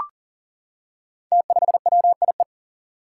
Morse code, heard as a keyed tone, about 37 wpm, 700 Hz. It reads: T5WIE
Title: Five Challenging Letter-Number Combinations 40wpm